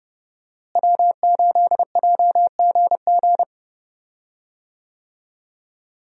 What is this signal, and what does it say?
Morse code, heard as a keyed tone, about 32 wpm, 700 Hz. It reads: W8JGG
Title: USA Call Signs - 5 Characters - Repeated 3x - 30wpm